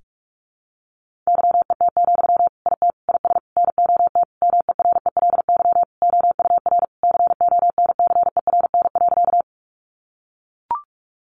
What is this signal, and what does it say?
Morse code, heard as a keyed tone, about 44 wpm, 700 Hz. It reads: YET, IT IS NOT MERELY OUR CONCERN.
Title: Sentences from Top 1400 Words 45wpm